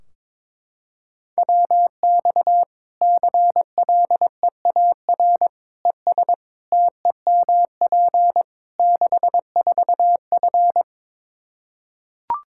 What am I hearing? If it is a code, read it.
Morse code, heard as a keyed tone, about 22 wpm, 700 Hz. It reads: WX CLEAR ES TEMP 64F
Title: K1USN SST QSO Practice 22wpm